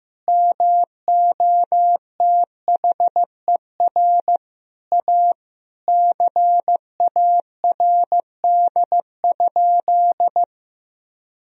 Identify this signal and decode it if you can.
Morse code, heard as a keyed tone, about 15 wpm, 700 Hz. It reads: MOTHER A CARD?
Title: Sentences from Top 1000 Words - Review Part 4 - 15wpm